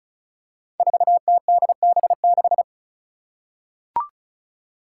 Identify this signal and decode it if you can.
Morse code, heard as a keyed tone, about 36 wpm, 700 Hz. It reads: 4TDB6
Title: Five Challenging Letter-Number Combinations 35wpm